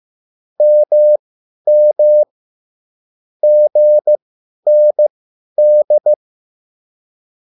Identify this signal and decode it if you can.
Morse code, heard as a keyed tone, about 15 wpm, spaced as for 10 wpm, 600 Hz. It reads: MM GND